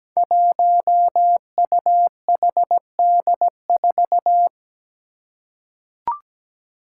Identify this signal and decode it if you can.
Morse code, heard as a keyed tone, about 17 wpm, 700 Hz. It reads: 1UHD4
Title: Five Challenging Letter-Number Combinations 17wpm